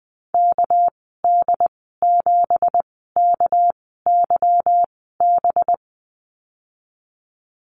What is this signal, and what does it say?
Morse code, heard as a keyed tone, about 20 wpm, spaced as for 14 wpm, 700 Hz. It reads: KD7KYB
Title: 200 Lesson 32g - Letter k - Call Signs 20wpm